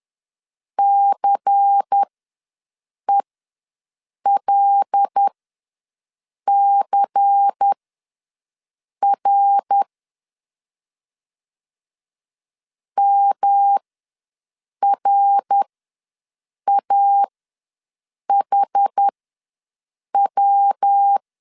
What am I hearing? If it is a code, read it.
Morse code, heard as a keyed tone, about 10 wpm, spaced as for 6 wpm, 800 Hz. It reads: CELCR MRAHW
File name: LEZIONI / 5 LBVC / 5 LBVC ws 10wpm 3733